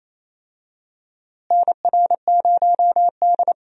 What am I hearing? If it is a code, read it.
Morse code, heard as a keyed tone, about 28 wpm, 700 Hz. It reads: NR0D
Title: USA Call Signs - 4 Characters - Repeated 3x - 28wpm